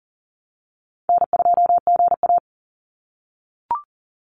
Morse code, heard as a keyed tone, about 40 wpm, 700 Hz. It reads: N2GA
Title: Easy Call Signs - 40wpm